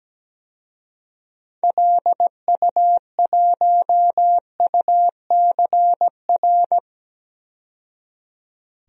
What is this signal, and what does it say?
Morse code, heard as a keyed tone, about 17 wpm, 700 Hz. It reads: LU1UCR
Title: DXCC-100 Call Signs - 6 Characters - 17wpm